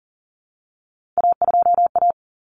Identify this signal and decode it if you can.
Morse code, heard as a keyed tone, about 40 wpm, 700 Hz. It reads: A2A
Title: Common Cut Numbers - 3-Digit 40wpm